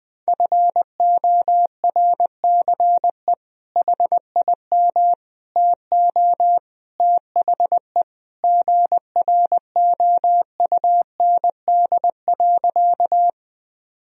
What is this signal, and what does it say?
Morse code, heard as a keyed tone, about 20 wpm, 700 Hz. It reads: FORCE HIM TO THE GROUND.
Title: Sentences from Top 1000 Words - Review Part 3 - 20wpm